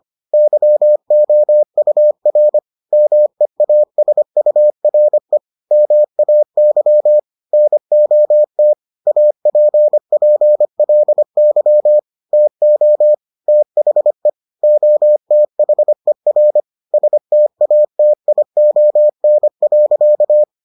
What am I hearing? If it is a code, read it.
Morse code, heard as a keyed tone, about 25 wpm, 600 Hz. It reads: YOUR MEASURE MAY NOT APPLY TO THE OTHER STATION.